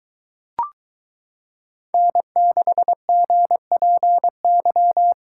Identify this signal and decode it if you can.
Morse code, heard as a keyed tone, about 22 wpm, 700 Hz. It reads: N6GPY